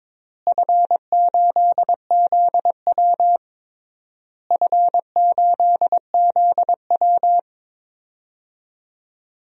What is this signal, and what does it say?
Morse code, heard as a keyed tone, about 21 wpm, 700 Hz. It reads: F8ZW F8ZW
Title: DXCC-100 Call Signs - 4 Characters - Repeated 3x - 22wpm